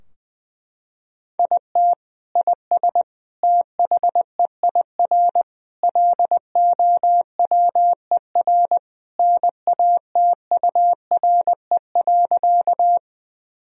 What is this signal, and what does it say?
Morse code, heard as a keyed tone, about 20 wpm, 700 Hz. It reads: IT IS THEIR LOWER NATURE.
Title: Sentences from Top 1000 Words - Review Part 4 - 20wpm